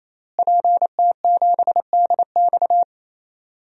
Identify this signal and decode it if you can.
Morse code, heard as a keyed tone, about 27 wpm, 700 Hz. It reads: PT7DX
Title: DXCC-100 Call Signs - 5 Characters - Repeated 3x - 28wpm